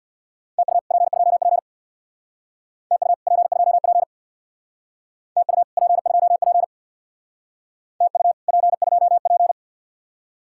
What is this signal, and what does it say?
Morse code, heard as a keyed tone, about 58 wpm, 700 Hz. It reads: TU P2P TU P2P TU P2P TU P2P
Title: Speed-Racing Common QSO Phrases - 50wpm